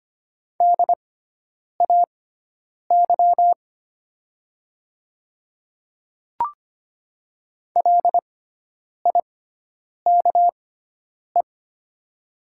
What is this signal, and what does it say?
Morse code, heard as a keyed tone, about 25 wpm, spaced as for 9 wpm, 690 Hz. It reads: DAY LIKE
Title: 6x Character Spacing - Top 300 Words - 25wpm